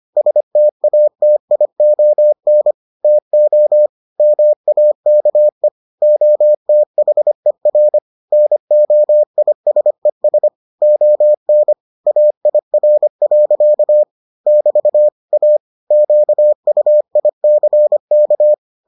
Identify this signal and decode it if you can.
Morse code, heard as a keyed tone, about 25 wpm, 600 Hz. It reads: STATION TO MAKE OTHER NOISES ON AIR. <BT> A QUICK